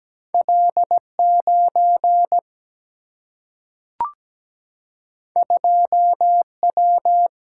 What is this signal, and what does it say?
Morse code, heard as a keyed tone, about 17 wpm, 700 Hz. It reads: L9 2W
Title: Single-Letter Single-Number 17wpm